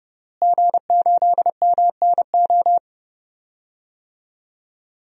Morse code, heard as a keyed tone, about 32 wpm, 700 Hz. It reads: G8MNO
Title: DXCC-100 Call Signs - 5 Characters - Repeated 2x - 30wpm